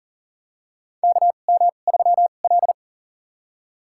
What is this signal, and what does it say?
Morse code, heard as a keyed tone, about 40 wpm, spaced as for 29 wpm, 700 Hz. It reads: KM3L
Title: USA Call Signs - 4 Characters - 2x Character Spacing - 40wpm